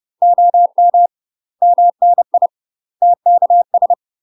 Morse code, heard as a keyed tone, about 30 wpm, 700 Hz. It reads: OM MNI TKS
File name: UK MASTER QSOs TRAINING / 30WPM / UK MASTER QSOs 0038